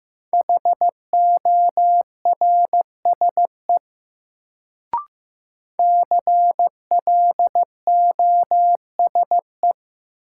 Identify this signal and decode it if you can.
Morse code, heard as a keyed tone, about 15 wpm, 700 Hz. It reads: HORSE CLOSE
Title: Sentences from Top 300 Words - New Word Drill - 15wpm